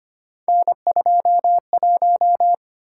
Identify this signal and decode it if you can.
Morse code, heard as a keyed tone, about 25 wpm, 700 Hz. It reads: N21
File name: Three Letter-Number Combinations - 25wpm